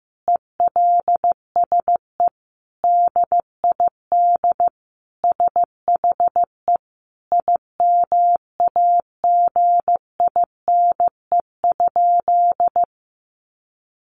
Morse code, heard as a keyed tone, about 15 wpm, 700 Hz. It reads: ELSE DID SHE IMAGINE?
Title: Sentences from Top 1400 Words 15wpm